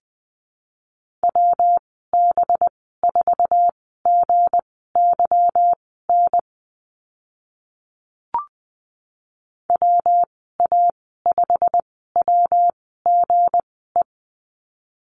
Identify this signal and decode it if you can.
Morse code, heard as a keyed tone, about 20 wpm, spaced as for 14 wpm, 700 Hz. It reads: WB4GYN WA5WGE
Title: 193 Lesson 31g - Letter v - Call Signs 20wpm